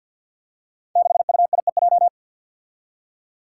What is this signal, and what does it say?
Morse code, heard as a keyed tone, about 52 wpm, 700 Hz. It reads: 6UIEJ
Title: Five Challenging Letter-Number Combinations 50wpm